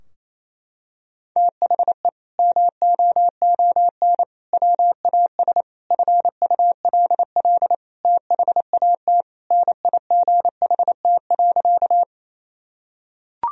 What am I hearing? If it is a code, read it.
Morse code, heard as a keyed tone, about 28 wpm, 700 Hz. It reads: THE MOON WAS FULL THAT NIGHT.
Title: Sentences from Top 1000 Words - Review Part 3 - 28wpm